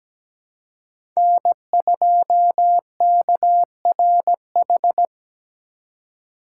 Morse code, heard as a keyed tone, about 17 wpm, 700 Hz. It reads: N2KRH
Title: USA Call Signs - 5 Characters - 2x Character Spacing - 17wpm